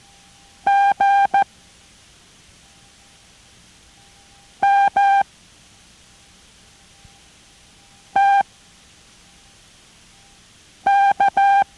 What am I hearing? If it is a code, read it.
Morse code, heard as a keyed tone, about 14 wpm, 800 Hz. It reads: G M T K